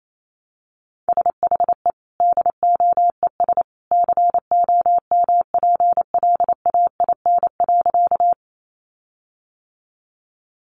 Morse code, heard as a keyed tone, about 28 wpm, 700 Hz. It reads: SHE DOES COMPLAIN.
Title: Sentences from Top 1300 Words 28wpm